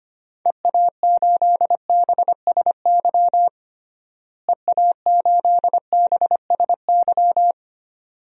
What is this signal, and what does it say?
Morse code, heard as a keyed tone, about 25 wpm, 700 Hz. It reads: EA8BSY EA8BSY
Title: DXCC-100 Call Signs - 6 Characters - Repeated 2x - 25wpm